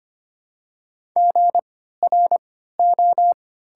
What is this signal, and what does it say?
Morse code, heard as a keyed tone, about 25 wpm, spaced as for 14 wpm, 700 Hz. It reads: GRO
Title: 3x Character Spacing - Top 500 Words - 25wpm